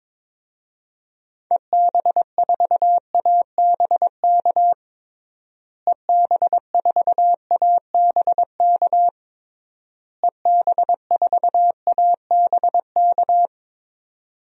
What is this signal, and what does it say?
Morse code, heard as a keyed tone, about 22 wpm, 700 Hz. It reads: EB4ABK EB4ABK EB4ABK
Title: DXCC-100 Call Signs - 6 Characters - Repeated 3x - 22wpm